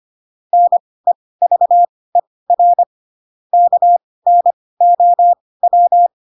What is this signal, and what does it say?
Morse code, heard as a keyed tone, about 25 wpm, spaced as for 18 wpm, 700 Hz. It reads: NEVER KNOW
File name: Pinocchio / Pinocchio ch7 12 25 ik1hge